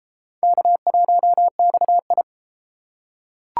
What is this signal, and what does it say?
Morse code, heard as a keyed tone, about 32 wpm, 700 Hz. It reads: K1XI